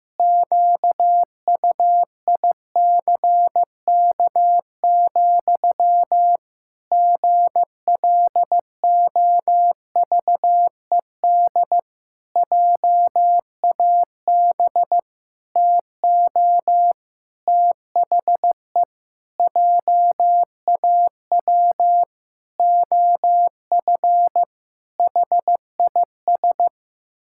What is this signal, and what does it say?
Morse code, heard as a keyed tone, about 15 wpm, 700 Hz. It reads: QUICK, GLOVED JAB TO THE JAW OF HIS
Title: Pangram Challenge v2 - 15wpm